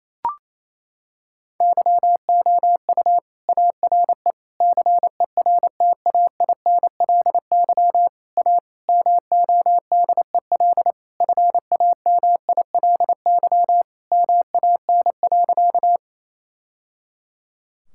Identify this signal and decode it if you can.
Morse code, heard as a keyed tone, about 28 wpm, 700 Hz. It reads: YOU ARE CERTAINLY A MODEL FAMILY MAN.
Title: Sentences from Top 1000 Words - Review Part 5 - 28wpm